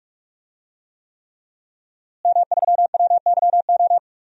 Morse code, heard as a keyed tone, about 46 wpm, 700 Hz. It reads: M3WYO
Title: DXCC-100 Call Signs - 5 Characters - Repeated 3x - 45wpm